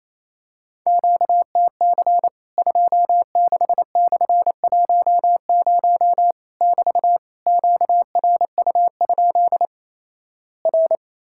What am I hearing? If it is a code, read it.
Morse code, heard as a keyed tone, about 28 wpm, 700 Hz. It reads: QTC 26/10 <BT> QRU? R
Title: Worked All Europe DX Contest - QTC Exchange - 28wpm